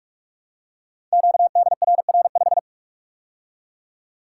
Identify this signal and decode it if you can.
Morse code, heard as a keyed tone, about 44 wpm, 700 Hz. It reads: QDRR5
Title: Code Groups - 5 Letter:Numbers - 45wpm